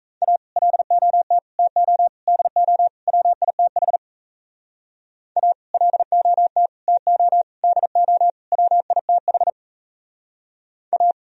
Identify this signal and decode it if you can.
Morse code, heard as a keyed tone, about 36 wpm, 700 Hz. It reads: A LOT TO DO WITH A LOT TO DO WITH A